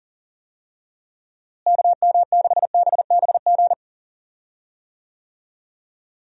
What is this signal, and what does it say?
Morse code, heard as a keyed tone, about 37 wpm, 700 Hz. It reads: KM6BBG
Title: USA Call Signs - 6 Characters - Repeated 2x - 40wpm